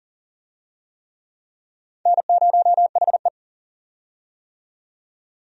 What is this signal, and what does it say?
Morse code, heard as a keyed tone, about 37 wpm, 700 Hz. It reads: N0HE
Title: USA Call Signs - 4 Characters - 40wpm